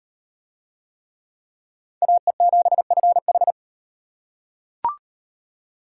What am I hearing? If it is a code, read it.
Morse code, heard as a keyed tone, about 35 wpm, 700 Hz. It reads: AE7FH